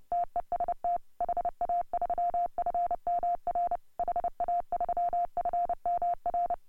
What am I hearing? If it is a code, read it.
Morse code, heard as a keyed tone, about 29 wpm, 690 Hz. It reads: TEST HA3FMR HA3FMR
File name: AUDIO2018 / HA3FMR CW